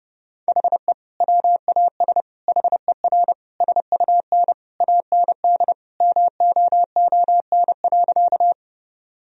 Morse code, heard as a keyed tone, about 30 wpm, 700 Hz. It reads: HE WAS HER SUN AND MOON.
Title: Sentences from Top 1000 Words - Review Part 3 - 30wpm